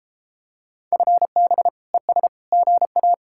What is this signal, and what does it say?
Morse code, heard as a keyed tone, about 33 wpm, 700 Hz. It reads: FB ES GA